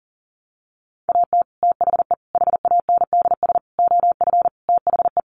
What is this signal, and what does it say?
Morse code, heard as a keyed tone, about 40 wpm, 700 Hz. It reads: AT THE HANDS OF THE
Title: Sets of 5 Words Encore2 40wpm